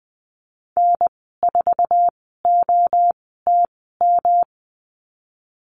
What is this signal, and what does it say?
Morse code, heard as a keyed tone, about 20 wpm, spaced as for 14 wpm, 700 Hz. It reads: N4OTM
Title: 140 Lesson 23g - Letter f - Call Signs 20wpm